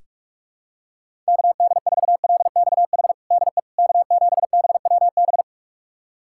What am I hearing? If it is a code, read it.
Morse code, heard as a keyed tone, about 43 wpm, 700 Hz. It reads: KD4LXH DE K7BWB
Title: USA Callsign DE Callsign 45wpm